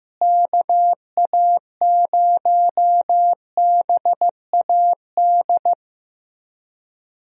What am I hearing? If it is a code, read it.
Morse code, heard as a keyed tone, about 15 wpm, 700 Hz. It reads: KA0BAD
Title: USA Call Signs - 6 Characters - Repeated 2x - 15wpm